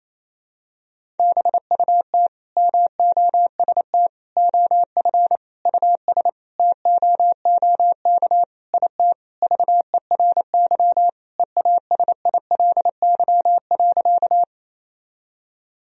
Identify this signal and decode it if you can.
Morse code, heard as a keyed tone, about 28 wpm, 700 Hz. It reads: BUT MOST OF US TOOK IT VERY EASILY.
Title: Sentences from Top 1000 Words - Review Part 1 - 28wpm